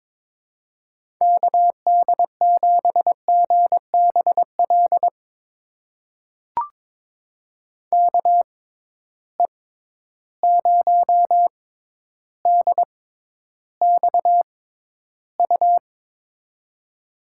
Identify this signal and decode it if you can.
Morse code, heard as a keyed tone, about 22 wpm, spaced as for 11 wpm, 700 Hz. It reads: KD7GBL KE0DXU
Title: USA Call Signs - 6 Characters - 6x Character Spacing - 22wpm